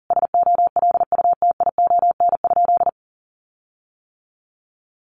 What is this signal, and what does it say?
Morse code, heard as a keyed tone, about 40 wpm, 700 Hz. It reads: SOLUTION?
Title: Sentences from Top 1300 Words 40wpm